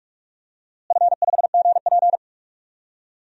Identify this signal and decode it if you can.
Morse code, heard as a keyed tone, about 43 wpm, 700 Hz. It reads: F5GP
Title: DXCC-100 Call Signs - 4 Characters - Repeated 2x - 45wpm